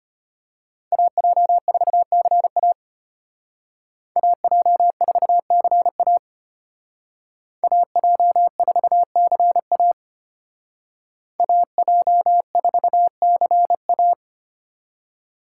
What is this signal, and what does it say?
Morse code, heard as a keyed tone, about 34 wpm, 700 Hz. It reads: AJ4CA AJ4CA AJ4CA AJ4CA